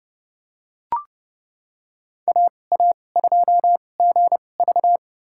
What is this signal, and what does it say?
Morse code, heard as a keyed tone, about 30 wpm, spaced as for 22 wpm, 700 Hz. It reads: AA2GV
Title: USA Call Signs - 5 Characters - 2x Character Spacing - 30wpm